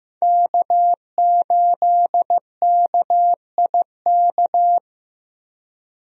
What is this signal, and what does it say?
Morse code, heard as a keyed tone, about 15 wpm, 700 Hz. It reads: K8KIK
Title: USA Call Signs - 5 Characters - Repeated 2x - 15wpm